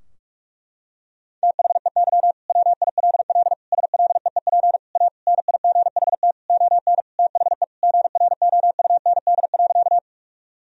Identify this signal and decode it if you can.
Morse code, heard as a keyed tone, about 46 wpm, 700 Hz. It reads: THEY WILL SLEEP A NIGHT ON THE GROUND.
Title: Sentences from Top 1000 Words - Review Part 2 - 45wpm